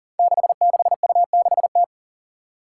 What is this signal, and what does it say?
Morse code, heard as a keyed tone, about 40 wpm, 700 Hz. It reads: 66U6T
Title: Five Challenging Letter-Number Combinations 40wpm